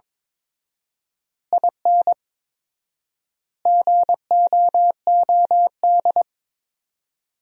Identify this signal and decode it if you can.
Morse code, heard as a keyed tone, about 21 wpm, 700 Hz. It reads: IN GOOD
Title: 4x Word-Spacing - Sentences from Top 200 Words - 22wpm